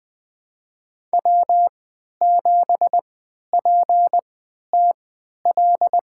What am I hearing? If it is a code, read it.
Morse code, heard as a keyed tone, about 20 wpm, spaced as for 11 wpm, 700 Hz. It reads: W7PTL
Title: USA Call Signs - 5 Characters - 3x Character Spacing - 20wpm